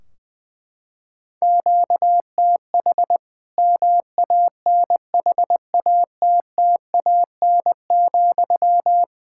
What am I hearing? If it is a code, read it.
Morse code, heard as a keyed tone, about 20 wpm, 700 Hz. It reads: QTH MANHATTAN,
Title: Practice QSO - Exchange 1 - Beginning - 20wpm